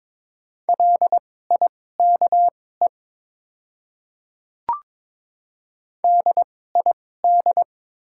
Morse code, heard as a keyed tone, about 22 wpm, spaced as for 16 wpm, 700 Hz. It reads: LIKE DID
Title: 2x Character Spacing - Top 100 Words - 22wpm